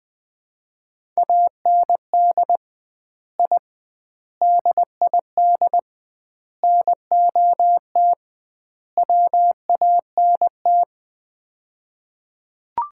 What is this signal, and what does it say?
Morse code, heard as a keyed tone, about 21 wpm, 700 Hz. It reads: AND I DID NOT WANT
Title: 2x Word-Spacing - Sets of 5 Words - 20wpm